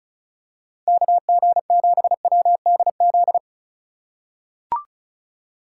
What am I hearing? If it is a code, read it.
Morse code, heard as a keyed tone, about 34 wpm, 700 Hz. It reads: KG7WDZ
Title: USA Call Signs - 6 Characters - 2x Character Spacing - 35wpm